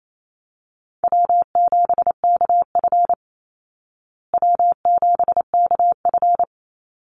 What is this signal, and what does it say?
Morse code, heard as a keyed tone, about 28 wpm, 700 Hz. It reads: W7KF W7KF
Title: Easy Call Signs Repeated 3x - 28wpm